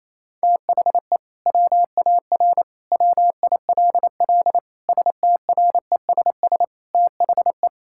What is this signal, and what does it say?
Morse code, heard as a keyed tone, about 28 wpm, 700 Hz. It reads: THE WAR WILL STRESS THE
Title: Sentences from Top 1000 Words - Review Part 6 - 28wpm